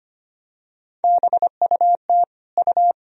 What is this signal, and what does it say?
Morse code, heard as a keyed tone, about 25 wpm, 700 Hz. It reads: BUT U
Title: Sentences from Top 300 Words and Abbreviations 25wpm